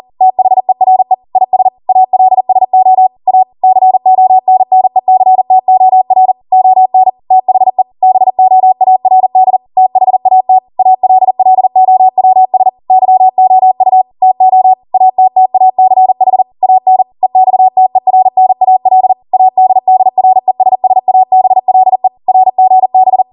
Morse code, heard as a keyed tone, about 40 wpm, 750 Hz. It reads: THERE IS ALSO A CONNECTOR ON THE BOARD THAT ALLOWS YOU TO ATTACH AN EXTERNAL ADDRESSABLE RGB